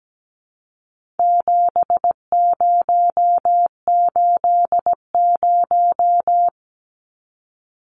Morse code, heard as a keyed tone, about 17 wpm, 700 Hz. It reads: 7080
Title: Numbers - 4-Digit 17wpm